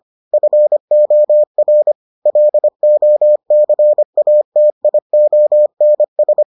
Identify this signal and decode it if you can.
Morse code, heard as a keyed tone, about 25 wpm, 600 Hz. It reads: FOR LOCATIONS